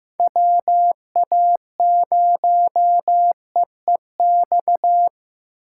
Morse code, heard as a keyed tone, about 15 wpm, 700 Hz. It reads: WA0EEX
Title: USA Call Signs - 6 Characters - 2x Character Spacing - 15wpm